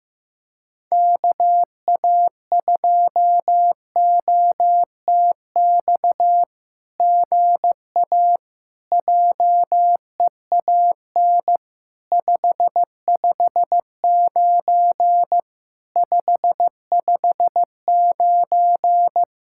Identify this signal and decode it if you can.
Morse code, heard as a keyed tone, about 15 wpm, 700 Hz. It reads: KA2OTX GA JEAN 559 559
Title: SOTA QSO Practice 15wpm